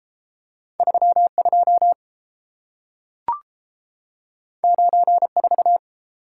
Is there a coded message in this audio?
Yes: Morse code from 0.8 to 5.8 s, heard as a keyed tone, about 30 wpm, 700 Hz. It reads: 32 94